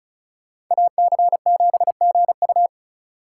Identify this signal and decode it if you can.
Morse code, heard as a keyed tone, about 46 wpm, 700 Hz. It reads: AC7GU